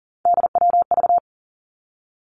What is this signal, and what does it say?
Morse code, heard as a keyed tone, about 40 wpm, 700 Hz. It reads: DWV
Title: Three Challenging Letter-Number Combinations 40wpm